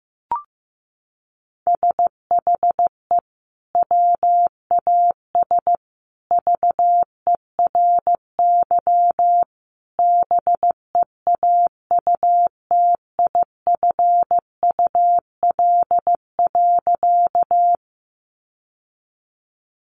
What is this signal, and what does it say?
Morse code, heard as a keyed tone, about 15 wpm, 700 Hz. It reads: SHE WAS VERY BEAUTIFUL.
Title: Sentences from Top 900 Words 15wpm